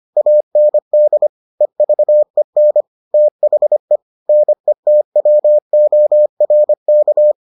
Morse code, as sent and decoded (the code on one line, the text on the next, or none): .- -. -.. / . ...- . -. / - .... / -. - .-- --- .-. -.-
AND EVEN THE NETWORK